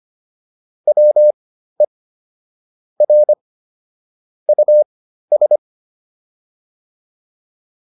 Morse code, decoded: WE R US

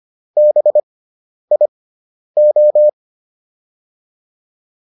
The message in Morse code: BIO

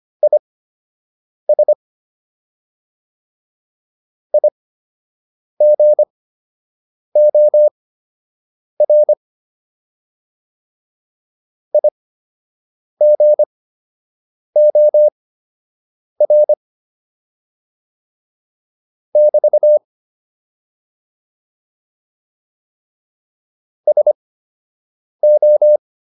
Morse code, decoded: IS IGOR IGOR <BT> SO

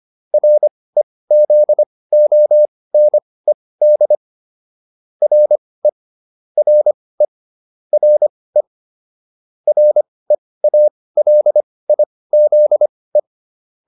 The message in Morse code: REZONED RE RE RE REALIZE

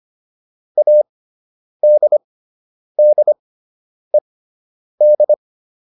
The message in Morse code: ADDED

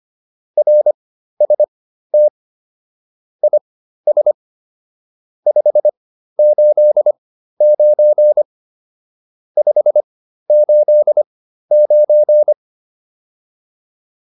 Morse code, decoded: RST IS 589 589

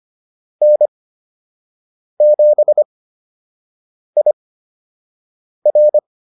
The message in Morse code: N7IR

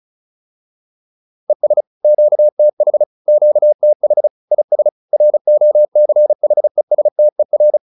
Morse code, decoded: ES QTH QTH IS ROCHESTER

